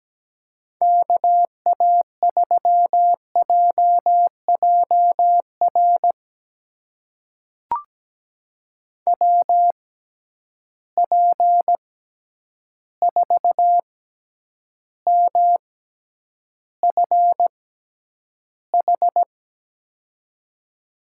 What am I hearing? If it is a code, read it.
Morse code, heard as a keyed tone, about 17 wpm, spaced as for 9 wpm, 700 Hz. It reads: KA3JJR WP4MFH
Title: USA Call Signs - 6 Characters - 6x Character Spacing - 17wpm